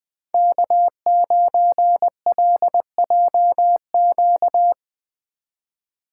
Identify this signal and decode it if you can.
Morse code, heard as a keyed tone, about 20 wpm, 700 Hz. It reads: K9LJQ
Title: USA Call Signs - 5 Characters - 2x Character Spacing - 20wpm